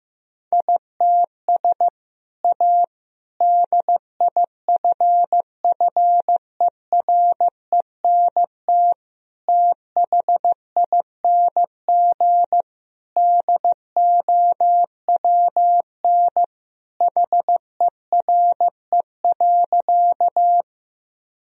Morse code, heard as a keyed tone, about 16 wpm, 700 Hz. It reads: ITS A DIFFERENT THING DOWN HERE.
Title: Sentences from Top 1000 Words - Review Part 3 - 15wpm